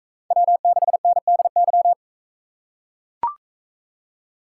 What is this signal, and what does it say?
Morse code, heard as a keyed tone, about 44 wpm, 700 Hz. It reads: W6NDY